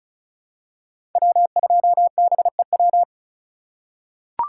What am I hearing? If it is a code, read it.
Morse code, heard as a keyed tone, about 34 wpm, 700 Hz. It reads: W2BEW
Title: USA Call Signs - 5 Characters - 4x Character Spacing - 35wpm